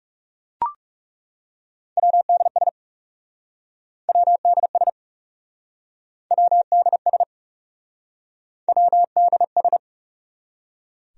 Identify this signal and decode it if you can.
Morse code, heard as a keyed tone, about 40 wpm, 700 Hz. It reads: WDS WDS WDS WDS